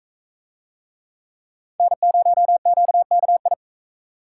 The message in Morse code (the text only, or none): N0QKI